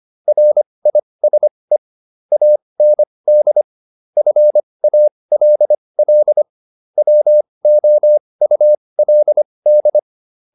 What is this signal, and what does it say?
Morse code, heard as a keyed tone, about 25 wpm, spaced as for 20 wpm, 600 Hz. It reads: RISE AND FALL WOULD